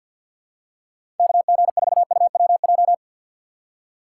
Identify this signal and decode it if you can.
Morse code, heard as a keyed tone, about 50 wpm, 700 Hz. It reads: KG4UWJ
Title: USA Call Signs - 6 Characters - 6x Character Spacing - 50wpm